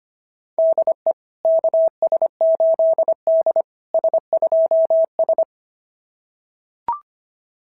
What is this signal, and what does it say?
Morse code, heard as a keyed tone, about 25 wpm, 690 Hz. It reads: DE KS8D S2S
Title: SOTA QSO Practice 25wpm